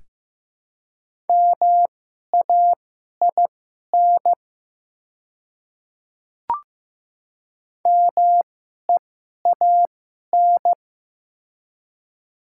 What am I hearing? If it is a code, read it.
Morse code, heard as a keyed tone, about 15 wpm, spaced as for 11 wpm, 700 Hz. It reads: MAIN MEAN
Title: 2x Character Spacing - Top 300 Words - 15wpm